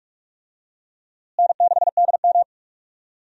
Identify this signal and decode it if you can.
Morse code, heard as a keyed tone, about 46 wpm, 700 Hz. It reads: N6DM
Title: USA Call Signs - 4 Characters - Repeated 3x - 45wpm